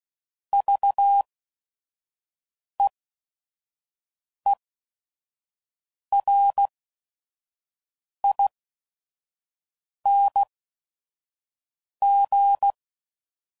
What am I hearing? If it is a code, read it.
Morse code, heard as a keyed tone, about 16 wpm, spaced as for 5 wpm, 800 Hz. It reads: VEERING
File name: Prefixes / By First Letter / Vs